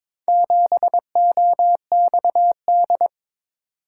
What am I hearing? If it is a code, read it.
Morse code, heard as a keyed tone, about 22 wpm, 700 Hz. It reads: 7OXD
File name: practice / 22wpm / Four Letter-Number Combinations - 22wpm